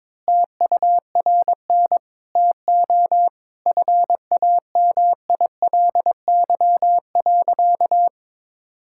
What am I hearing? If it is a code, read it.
Morse code, heard as a keyed tone, about 22 wpm, 700 Hz. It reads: TURN TO FAMILY.
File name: practice / 22wpm / Sentences with 3-Word Phrases - 22wpm